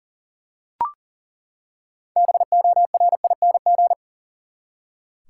Morse code, heard as a keyed tone, about 40 wpm, 700 Hz. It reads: BORING
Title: Six Letter Words - 40wpm